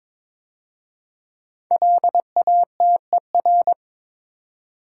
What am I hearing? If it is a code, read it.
Morse code, heard as a keyed tone, about 22 wpm, 700 Hz. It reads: LATER
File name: 5 Letter Words - First 3 Letters Given - 22wpm